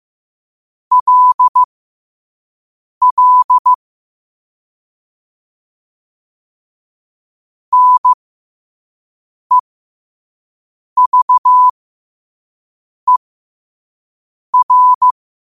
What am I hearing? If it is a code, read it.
Morse code, heard as a keyed tone, about 15 wpm, spaced as for 5 wpm, 1000 Hz. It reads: LL NEVER
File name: qotdfiles / 5wpm